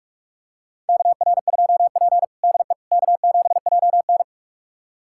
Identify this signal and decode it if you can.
Morse code, heard as a keyed tone, about 44 wpm, 690 Hz. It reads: KR2P DE K7JN
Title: USA Callsign DE Callsign - 4 Characters - 45wpm